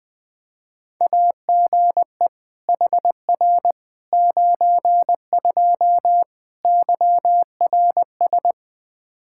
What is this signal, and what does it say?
Morse code, heard as a keyed tone, about 20 wpm, 700 Hz. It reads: AGE HR 92 YRS
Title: K1USN SST QSO Practice 20wpm